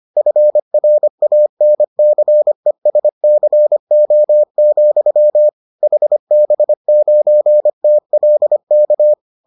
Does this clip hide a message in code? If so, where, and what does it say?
Yes: Morse code from 0.2 to 9.1 s, heard as a keyed tone, about 25 wpm, 600 Hz. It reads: FRANCESCO, HB9TLK